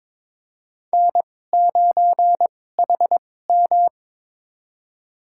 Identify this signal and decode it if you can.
Morse code, heard as a keyed tone, about 22 wpm, spaced as for 16 wpm, 700 Hz. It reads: N9HM
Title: USA Call Signs - 4 Characters - 2x Character Spacing - 22wpm